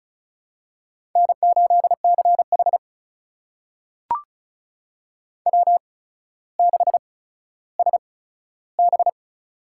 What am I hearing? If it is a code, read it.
Morse code, heard as a keyed tone, about 36 wpm, spaced as for 15 wpm, 700 Hz. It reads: N8CH W6SB
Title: USA Call Signs - 4 Characters - 8x Character Spacing - 35wpm